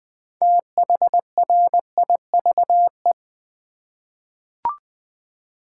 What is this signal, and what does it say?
Morse code, heard as a keyed tone, about 20 wpm, 700 Hz. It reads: THRIVE
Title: 6 Letter Words - First 3 Letters Given - 20wpm